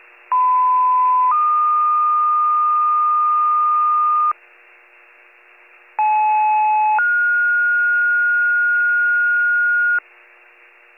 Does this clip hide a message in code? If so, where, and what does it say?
Yes: two-tone page from 0.3 to 10.0 s, heard as a one-second tone, then a three-second tone, about 1010 then 1220 Hz, about 860 then 1430 Hz.